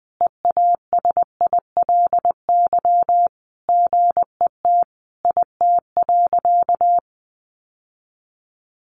Morse code, heard as a keyed tone, about 19 wpm, 700 Hz. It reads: EASILY GET IT.
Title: Sentences from Top 1500 Words 20wpm